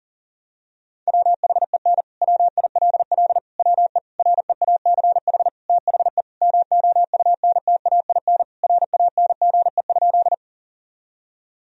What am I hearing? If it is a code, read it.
Morse code, heard as a keyed tone, about 40 wpm, 700 Hz. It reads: WHEN WILL WE REACH THE MOUNTAIN RANGE?
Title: Sentences from Top 1000 Words - Review Part 4 - 40wpm